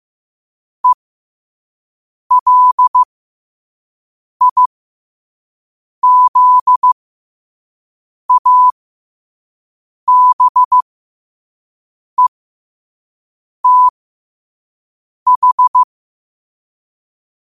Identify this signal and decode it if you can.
Morse code, heard as a keyed tone, about 15 wpm, spaced as for 5 wpm, 1000 Hz. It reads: ELIZABETH